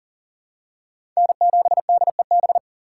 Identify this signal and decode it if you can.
Morse code, heard as a keyed tone, about 40 wpm, 700 Hz. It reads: N7DEB
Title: USA Call Signs - 5 Characters - 4x Character Spacing - 40wpm